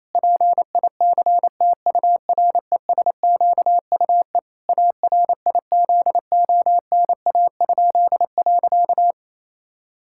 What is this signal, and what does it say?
Morse code, heard as a keyed tone, about 28 wpm, 700 Hz. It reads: PICTURESQUE ARIZONA?.
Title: Pangram Challenge v2 - 28wpm